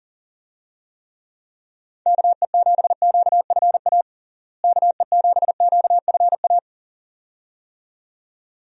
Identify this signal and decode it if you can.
Morse code, heard as a keyed tone, about 40 wpm, 700 Hz. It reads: KE7QFA KE7QFA
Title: USA Call Signs - 6 Characters - Repeated 2x - 40wpm